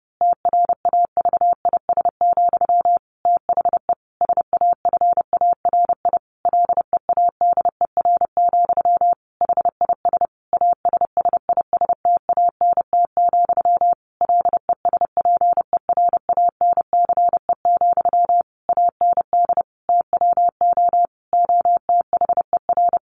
Morse code, heard as a keyed tone, about 30 wpm, 700 Hz. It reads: TRAVIS, THE SAFARI LEADER, HIS ASSISTANT, LESPERANCE, AND TWO OTHER
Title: A Sound Of Thunder 30wpm